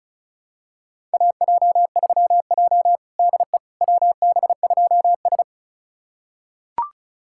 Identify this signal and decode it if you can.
Morse code, heard as a keyed tone, about 36 wpm, 700 Hz. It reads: AJ3J DE WB2S
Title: USA Callsign DE Callsign - 4 Characters - 35wpm